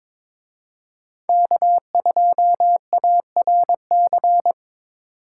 Morse code, heard as a keyed tone, about 26 wpm, 700 Hz. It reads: K2ARC
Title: Speed Racing - USA Call Signs 22wpm